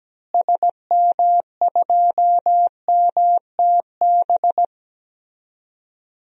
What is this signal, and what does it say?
Morse code, heard as a keyed tone, about 17 wpm, 700 Hz. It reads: SM2MTB
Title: DXCC-100 Call Signs - 6 Characters - Repeated 2x - 17wpm